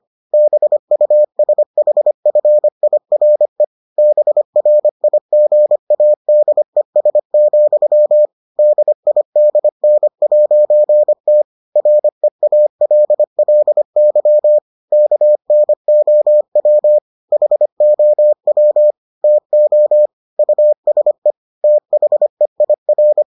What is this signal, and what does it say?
Morse code, heard as a keyed tone, about 25 wpm, 600 Hz. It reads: BUSHFIRE BRIGADES, DIDN'T REALLY KNOW HOW TO USE THEIR